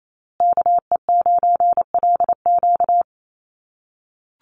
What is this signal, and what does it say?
Morse code, heard as a keyed tone, about 28 wpm, 700 Hz. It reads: KE9LQ
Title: USA Call Signs Repeated 3x 28wpm